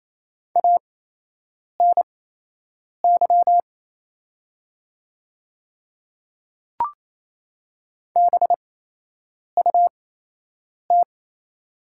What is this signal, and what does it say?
Morse code, heard as a keyed tone, about 28 wpm, spaced as for 8 wpm, 700 Hz. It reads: ANY BUT
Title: 8x Character Spacing - Top 200 Words - 28wpm